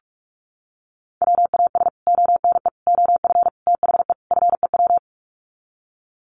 Morse code, heard as a keyed tone, about 44 wpm, 700 Hz. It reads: WAS ONE OF THE FEW
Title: Sets of 5 Words Encore 45wpm